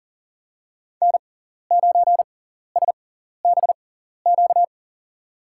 Morse code, heard as a keyed tone, about 40 wpm, spaced as for 14 wpm, 700 Hz. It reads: N9SBQ
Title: USA Call Signs - 5 Characters - 6x Character Spacing - 40wpm